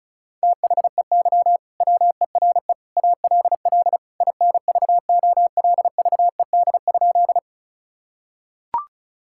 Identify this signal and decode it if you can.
Morse code, heard as a keyed tone, about 36 wpm, 700 Hz. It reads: THEY WERE ALL INVOLVED?
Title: Sentences from Top 1000 Words - Review Part 5 - 35wpm